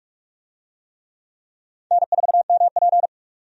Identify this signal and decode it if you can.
Morse code, heard as a keyed tone, about 46 wpm, 700 Hz. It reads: N4MP
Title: USA Call Signs - 4 Characters - 45wpm